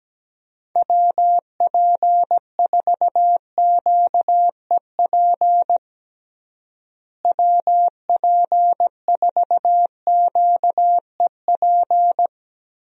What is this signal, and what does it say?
Morse code, heard as a keyed tone, about 17 wpm, 700 Hz. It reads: WP4QEP WP4QEP
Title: USA Call Signs - 6 Characters - Repeated 3x - 17wpm